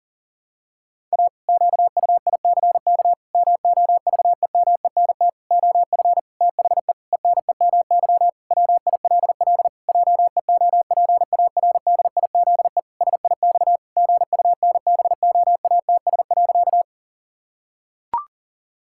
Morse code, heard as a keyed tone, about 37 wpm, 700 Hz. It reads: A QUICK MOVEMENT OF THE ENEMY WILL JEOPARDIZE SIX GUNBOATS.
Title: Pangram Challenge v2 - 40wpm